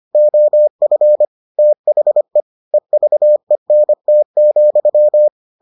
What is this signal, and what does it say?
Morse code, heard as a keyed tone, about 25 wpm, 600 Hz. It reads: OF THE EVENT,